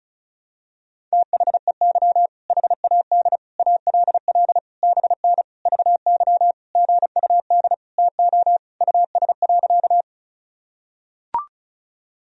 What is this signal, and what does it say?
Morse code, heard as a keyed tone, about 34 wpm, 700 Hz. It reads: THEY HAD ALL BN VY GUD TO US.
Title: Sentences from Top 200 Words and Abbreviations 35wpm